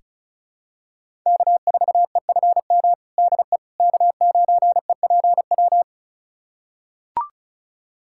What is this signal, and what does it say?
Morse code, heard as a keyed tone, about 36 wpm, 700 Hz. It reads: K4EFM DE K9EPW
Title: USA Callsign DE Callsign - 5 Characters - 35wpm